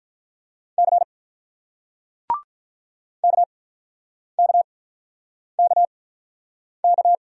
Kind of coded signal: Morse code

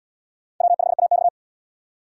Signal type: Morse code

